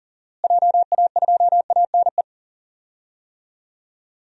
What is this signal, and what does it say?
Morse code, heard as a keyed tone, about 40 wpm, 700 Hz. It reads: JA2ANE
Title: DXCC-100 Call Signs - 6 Characters - Repeated 3x - 40wpm